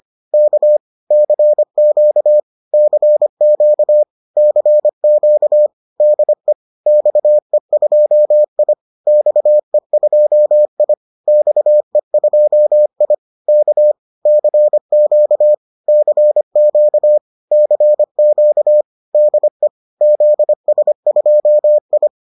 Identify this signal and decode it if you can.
Morse code, heard as a keyed tone, about 25 wpm, 600 Hz. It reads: K CQ CQ CQ DE XE2I XE2I XE2I K CQ CQ CQ DE ZS2I